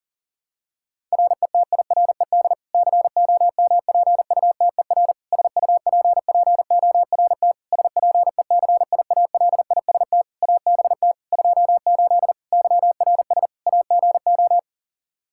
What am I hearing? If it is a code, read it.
Morse code, heard as a keyed tone, about 37 wpm, 700 Hz. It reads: RETIRED COMPUTER SUPPORT SPECIALIST ABT 28 YRS AGO
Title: Practice QSO - Exchange 3 - More Info - 40wpm